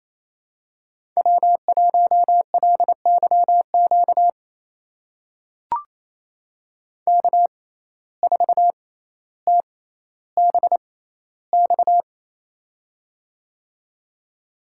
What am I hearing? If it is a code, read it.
Morse code, heard as a keyed tone, about 27 wpm, spaced as for 14 wpm, 700 Hz. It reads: W1LYQ K4TBX
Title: USA Call Signs - 5 Characters - 6x Character Spacing - 28wpm